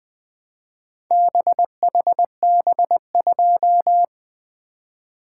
Morse code, heard as a keyed tone, about 21 wpm, 700 Hz. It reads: BHB2
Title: Four Letter-Number Combinations - 20wpm